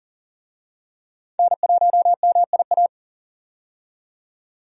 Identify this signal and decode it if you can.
Morse code, heard as a keyed tone, about 38 wpm, 700 Hz. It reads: N1MIA
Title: USA Call Signs - 5 Characters - 40wpm